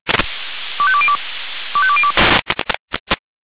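Five-tone selective call (ZVEI), heard as quick runs of five tones, tones of about 70 ms. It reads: 25202 25202
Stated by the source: Five-tone ZVEI selective call received on an SDR.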